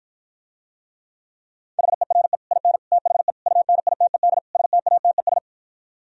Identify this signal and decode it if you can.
Morse code, heard as a keyed tone, about 52 wpm, 700 Hz. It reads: HERE IN THE UNITED STATES